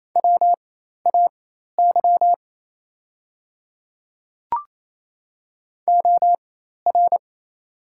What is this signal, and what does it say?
Morse code, heard as a keyed tone, about 28 wpm, spaced as for 13 wpm, 700 Hz. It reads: WAY OR
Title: 4x Character Spacing - Top 100 Words - 28wpm